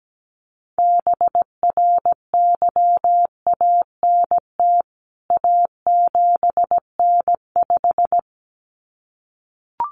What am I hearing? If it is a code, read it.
Morse code, heard as a keyed tone, about 17 wpm, 700 Hz. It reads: BRYANT A7N5
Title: CWT Exchange 17wpm